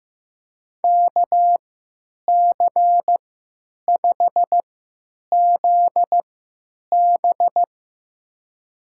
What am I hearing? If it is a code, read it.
Morse code, heard as a keyed tone, about 15 wpm, spaced as for 9 wpm, 700 Hz. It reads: KC5ZB
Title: USA Call Signs - 5 Characters - 3x Character Spacing - 15wpm